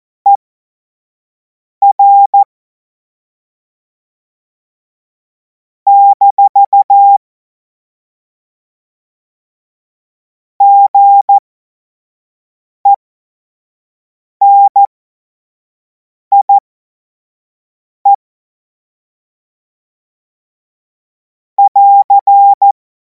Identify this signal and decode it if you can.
Morse code, heard as a keyed tone, about 14 wpm, spaced as for 5 wpm, 800 Hz. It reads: ER - GENIE <AR>
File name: Morsekurs / Witze / Witz 02. Juni 2026 25 F70